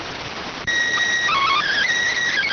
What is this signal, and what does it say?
SSTV (slow-scan television), PD 290.